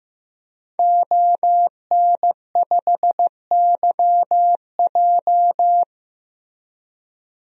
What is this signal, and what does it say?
Morse code, heard as a keyed tone, about 15 wpm, 700 Hz. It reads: ON5YJ
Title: DXCC-100 Call Signs - 5 Characters - Repeated 3x - 15wpm